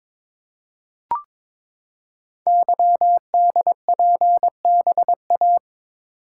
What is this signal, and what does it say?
Morse code, heard as a keyed tone, about 22 wpm, 700 Hz. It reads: YDPBA
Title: Code Groups - 5 Letters - 22wpm